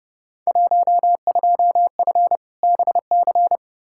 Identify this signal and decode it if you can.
Morse code, heard as a keyed tone, about 30 wpm, 700 Hz. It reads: 12F BC
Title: Field Day Exchanges Encore - 30wpm